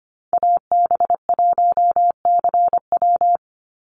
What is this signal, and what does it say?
Morse code, heard as a keyed tone, about 25 wpm, 700 Hz. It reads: AB1CW
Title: Call Signs Repeated 2x - 25wpm